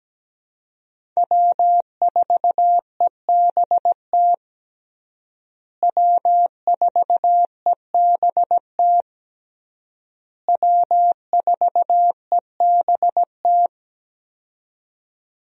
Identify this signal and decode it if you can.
Morse code, heard as a keyed tone, about 17 wpm, 700 Hz. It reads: W4EBT W4EBT W4EBT
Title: USA Call Signs - 5 Characters - Repeated 3x - 17wpm